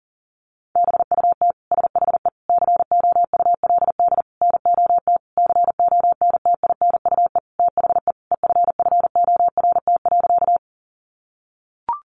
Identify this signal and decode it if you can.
Morse code, heard as a keyed tone, about 40 wpm, 700 Hz. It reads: BUT SHE COULD NOT CONTINUE THE EFFORT.
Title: Sentences from Top 1200 Words 40wpm